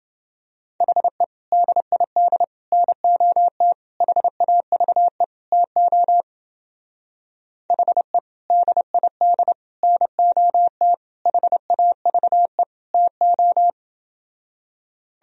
Morse code, heard as a keyed tone, about 28 wpm, 700 Hz. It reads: HE DID NOT HAVE TO HE DID NOT HAVE TO